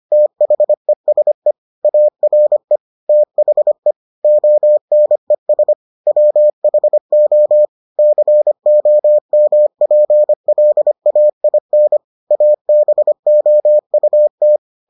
Morse code, heard as a keyed tone, about 25 wpm, 600 Hz. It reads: THESE ARE THE ONES WHO COMPLAIN ABOUT